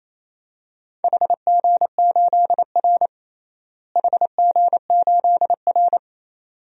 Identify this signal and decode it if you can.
Morse code, heard as a keyed tone, about 28 wpm, 700 Hz. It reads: HG8R HG8R
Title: DXCC-100 Call Signs - 4 Characters - Repeated 2x - 28wpm